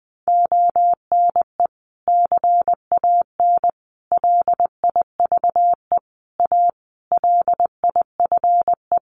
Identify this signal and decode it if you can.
Morse code, heard as a keyed tone, about 20 wpm, 700 Hz. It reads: ONE CAN LIVE A LIFE
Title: Sentences from Top 1400 Words 20wpm